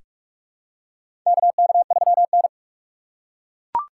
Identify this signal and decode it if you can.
Morse code, heard as a keyed tone, about 42 wpm, 700 Hz. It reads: KK3N
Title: USA Call Signs - 4 Characters - 3x Character Spacing - 45wpm